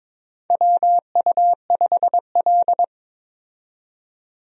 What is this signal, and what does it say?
Morse code, heard as a keyed tone, about 22 wpm, 700 Hz. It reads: WU5L
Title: USA Call Signs - 4 Characters - 22wpm